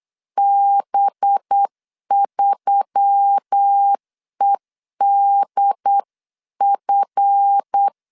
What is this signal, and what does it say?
Morse code, heard as a keyed tone, about 8 wpm, 800 Hz. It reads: B3EDF
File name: LEZIONI / 9 286 8wpm 3425